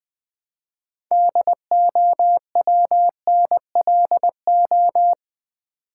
Morse code, heard as a keyed tone, about 20 wpm, 700 Hz. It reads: DOWNLO
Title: Build A Word - 3 Letters At A Time - 8-Letter Words - 20wpm